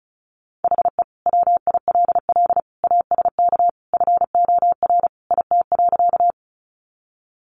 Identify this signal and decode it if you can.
Morse code, heard as a keyed tone, about 34 wpm, 700 Hz. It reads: HE WILL ASK FOR IT.
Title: Sentences from Top 200 Words 35wpm